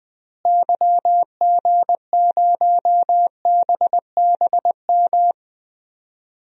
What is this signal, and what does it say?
Morse code, heard as a keyed tone, about 20 wpm, 700 Hz. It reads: YG0BBM
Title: DXCC-100 Call Signs - 6 Characters - Repeated 3x - 20wpm